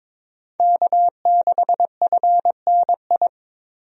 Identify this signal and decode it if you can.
Morse code, heard as a keyed tone, about 22 wpm, 700 Hz. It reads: K6FNI
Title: USA Call Signs - 5 Characters - 22wpm